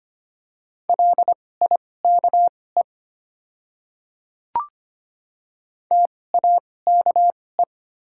Morse code, heard as a keyed tone, about 25 wpm, spaced as for 18 wpm, 700 Hz. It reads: LIKE TAKE
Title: 2x Character Spacing - Top 200 Words - 25wpm